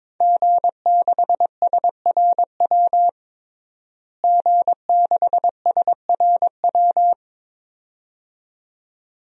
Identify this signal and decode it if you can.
Morse code, heard as a keyed tone, about 22 wpm, 700 Hz. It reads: G6SRW G6SRW
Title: DXCC-100 Call Signs - 5 Characters - Repeated 2x - 22wpm